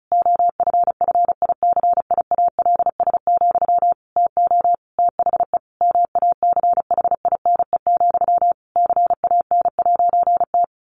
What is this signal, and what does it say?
Morse code, heard as a keyed tone, about 34 wpm, 700 Hz. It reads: OFFICIALS, TO THE MACHINE, CAN'T
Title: A Sound Of Thunder 35wpm